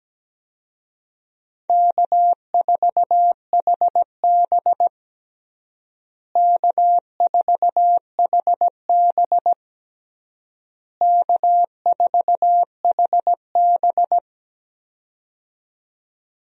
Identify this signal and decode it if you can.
Morse code, heard as a keyed tone, about 17 wpm, 700 Hz. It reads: K4HB K4HB K4HB
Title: USA Call Signs - 4 Characters - Repeated 3x - 17wpm